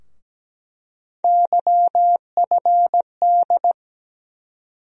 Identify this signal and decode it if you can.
Morse code, heard as a keyed tone, about 17 wpm, 700 Hz. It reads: YFD
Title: USA Call Sign Postfixes - 3 Letters 17wpm